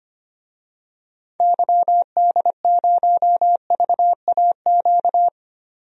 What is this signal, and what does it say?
Morse code, heard as a keyed tone, about 25 wpm, 700 Hz. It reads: YD0VAQ
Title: DXCC-100 Call Signs - 6 Characters - 25wpm